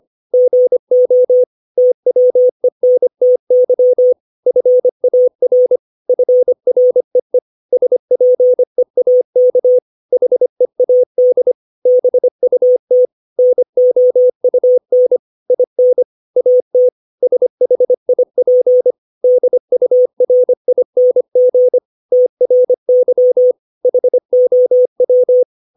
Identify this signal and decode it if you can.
Morse code, heard as a keyed tone, about 25 wpm, 500 Hz. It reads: GO TWENTY FAR FREE SPEAK HEAD BUT NOUN IN AT SHIP DURING TRY HOW
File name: 632 words / 632 Words 25WPM